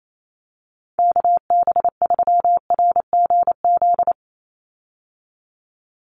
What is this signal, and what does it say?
Morse code, heard as a keyed tone, about 28 wpm, 700 Hz. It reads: KB3RGZ
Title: USA Call Signs 28wpm